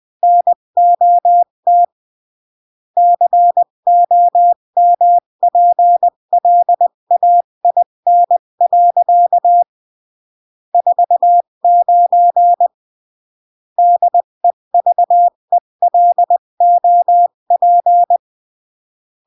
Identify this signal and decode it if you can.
Morse code, heard as a keyed tone, about 20 wpm, 700 Hz. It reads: NOT COMPLAIN. 49 DEVELOP
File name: CW / jodorowsky / 82 commandments 20-15